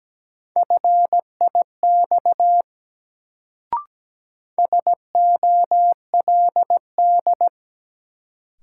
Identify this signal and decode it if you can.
Morse code, heard as a keyed tone, about 17 wpm, 700 Hz. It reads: FIX SOLD
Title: Sentences from Top 1400 Words - New Word Drill - 17wpm